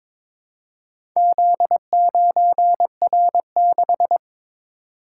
Morse code, heard as a keyed tone, about 22 wpm, 700 Hz. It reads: Z9R6
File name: practice / 22wpm / Four Letter-Number Combinations - 22wpm